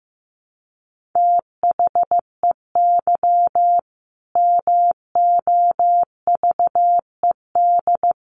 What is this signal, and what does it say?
Morse code, heard as a keyed tone, about 15 wpm, 700 Hz. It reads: THEY MOVED
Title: A Sound Of Thunder 15wpm